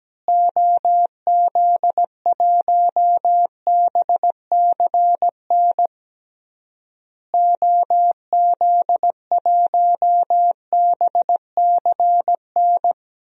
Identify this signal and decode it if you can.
Morse code, heard as a keyed tone, about 17 wpm, 700 Hz. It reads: OZ1BCN OZ1BCN
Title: DXCC-100 Call Signs - 6 Characters - Repeated 3x - 17wpm